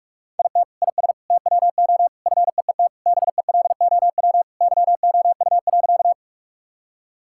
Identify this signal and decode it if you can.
Morse code, heard as a keyed tone, about 41 wpm, 700 Hz. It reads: IT IS TWO FEET BELOW YOU.
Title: Sentences from Top 1000 Words - Review Part 5 - 45wpm